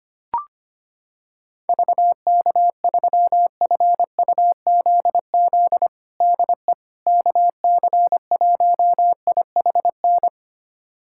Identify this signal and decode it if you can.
Morse code, heard as a keyed tone, about 25 wpm, 700 Hz. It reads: VK3FUZZ DE KC1IHN
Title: USA Callsign DE Callsign 25wpm